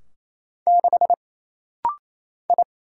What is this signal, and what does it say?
Morse code, heard as a keyed tone, about 27 wpm, 700 Hz. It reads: 6 I
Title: Single Letter-Number - Rapid-Fire 28wpm